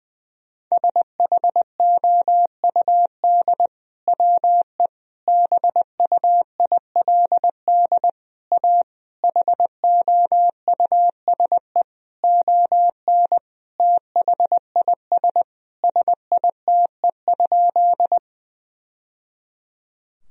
Morse code, heard as a keyed tone, about 20 wpm, 700 Hz. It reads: SHOUD WE BUILD A HOUSE ON THIS SITE?
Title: Sentences from Top 1000 Words - Review Part 4 - 20wpm